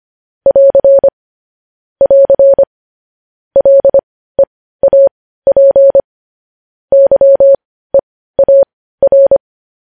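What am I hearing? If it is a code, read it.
Morse code, heard as a keyed tone, about 25 wpm, spaced as for 15 wpm, 550 Hz. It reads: <AR> <AR> LEAP YEAR
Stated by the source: Morse code news headlines at 15.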